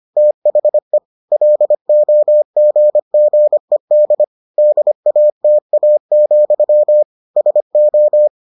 Morse code, heard as a keyed tone, about 25 wpm, 600 Hz. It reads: THE LOGGED DATA, SO